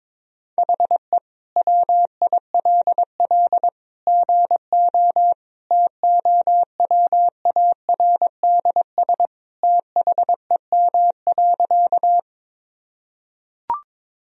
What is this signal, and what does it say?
Morse code, heard as a keyed tone, about 22 wpm, 700 Hz. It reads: HE WILL GO TOWARDS THEM.
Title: Sentences from Top 1000 Words - Review Part 2 - 22wpm